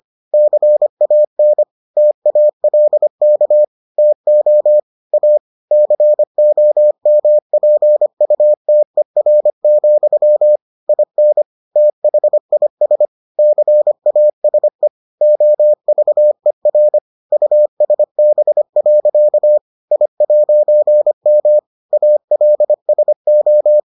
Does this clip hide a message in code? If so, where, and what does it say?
Yes: Morse code from 0.3 to 23.8 s, heard as a keyed tone, about 25 wpm, 600 Hz. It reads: CAN TALK TO A COMPUTER, IN THIS CASE OVER USB. I'M ALSO